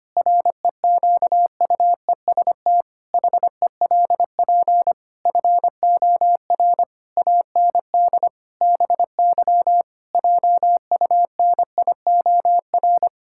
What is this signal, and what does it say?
Morse code, heard as a keyed tone, about 25 wpm, 700 Hz. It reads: REQUEST HELP FOR AND BY JUNIOR
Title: Pangram Challenge v2 - 25wpm